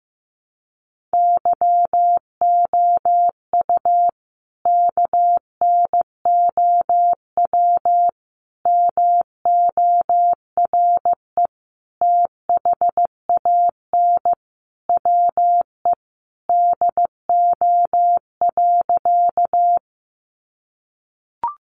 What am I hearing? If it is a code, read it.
Morse code, heard as a keyed tone, about 15 wpm, 700 Hz. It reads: YOU KNOW MORE THAN WE DO.
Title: Sentences from Top 100 Words 15wpm